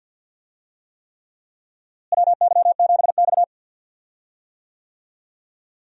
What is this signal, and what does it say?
Morse code, heard as a keyed tone, about 51 wpm, 700 Hz. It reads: WY7X
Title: USA Call Signs - 4 Characters - Repeated 3x - 50wpm